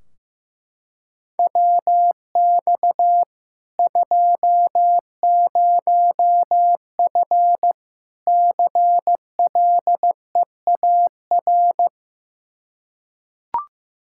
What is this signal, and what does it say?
Morse code, heard as a keyed tone, about 15 wpm, 700 Hz. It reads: WX 20F CLEAR
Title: K1USN SST QSO Practice 15wpm